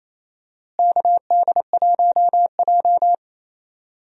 Morse code, heard as a keyed tone, about 28 wpm, 700 Hz. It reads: KD1J
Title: USA Call Signs - 4 Characters - Repeated 2x - 28wpm